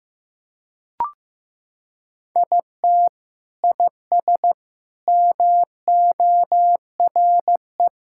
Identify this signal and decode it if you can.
Morse code, heard as a keyed tone, about 15 wpm, 700 Hz. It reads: IT IS MORE
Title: Sentences from Top 1000 Words - Review Part 5 - 15wpm